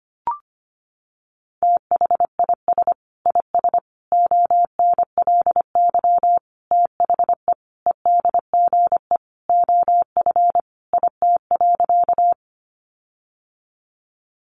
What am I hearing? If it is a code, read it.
Morse code, heard as a keyed tone, about 25 wpm, 700 Hz. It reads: THIS IS ONLY THE EDGE OF IT.
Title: Sentences from Top 1300 Words 25wpm